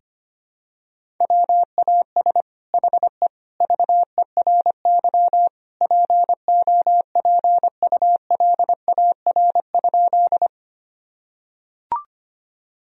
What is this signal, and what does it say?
Morse code, heard as a keyed tone, about 25 wpm, 700 Hz. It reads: WAS HE VERY POPULAR?
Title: Sentences from Top 1000 Words - Review Part 3 - 25wpm